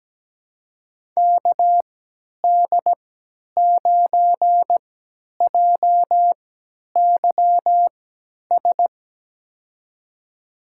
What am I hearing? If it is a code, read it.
Morse code, heard as a keyed tone, about 17 wpm, spaced as for 10 wpm, 700 Hz. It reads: KD9JYS
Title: USA Call Signs - 6 Characters - 3x Character Spacing - 17wpm